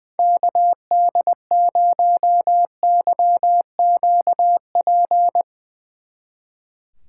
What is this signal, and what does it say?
Morse code, heard as a keyed tone, about 22 wpm, 700 Hz. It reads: KD0YQP
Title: Speed Racing - USA 2x3 Call Signs 20wpm